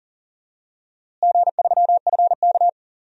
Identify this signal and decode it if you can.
Morse code, heard as a keyed tone, about 40 wpm, 700 Hz. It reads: G3FK
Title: DXCC-100 Call Signs - 4 Characters - Repeated 2x - 40wpm